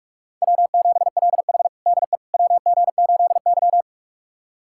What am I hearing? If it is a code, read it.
Morse code, heard as a keyed tone, about 44 wpm, 700 Hz. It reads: W7LH DE WG8Y
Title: USA Callsign DE Callsign - 4 Characters - 45wpm